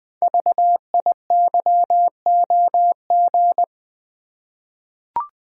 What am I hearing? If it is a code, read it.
Morse code, heard as a keyed tone, about 20 wpm, 700 Hz. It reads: VIYOG
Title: Code Groups - 5 Letters - 20wpm